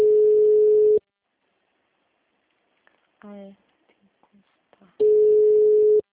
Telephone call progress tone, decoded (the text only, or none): ringback tone